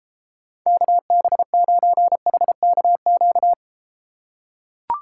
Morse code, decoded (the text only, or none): KB9HKQ